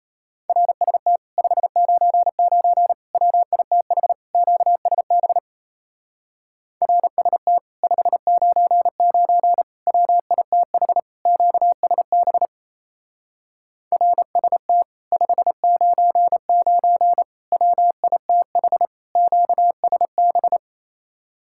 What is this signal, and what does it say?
Morse code, heard as a keyed tone, about 33 wpm, 700 Hz. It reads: RST 599 WITH QSB RST 599 WITH QSB RST 599 WITH QSB
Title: Speed-Racing Common QSO Phrases - 28wpm